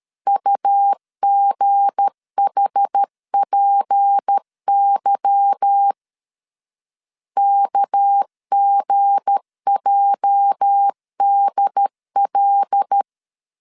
Morse code, heard as a keyed tone, about 13 wpm, 800 Hz. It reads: UGHPY KGJDL
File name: CW / DET / 1 9 1500C S B 12wpm 3819